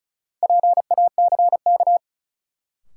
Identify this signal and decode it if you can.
Morse code, heard as a keyed tone, about 36 wpm, 700 Hz. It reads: PACK
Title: Sentences from Top 1300 Words - New Word Drill - 35wpm